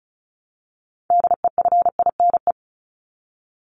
Morse code, heard as a keyed tone, about 34 wpm, 700 Hz. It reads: DEFINE
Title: ED Suffix - Top 100 Words with Base Words 35wpm